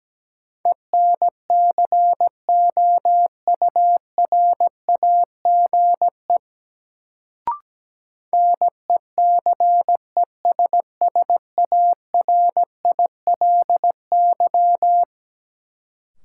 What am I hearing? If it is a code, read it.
Morse code, heard as a keyed tone, about 17 wpm, 700 Hz. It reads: ENCOURAGE NECESSARILY
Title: Sentences from Top 1400 Words - New Word Drill Alt - 17wpm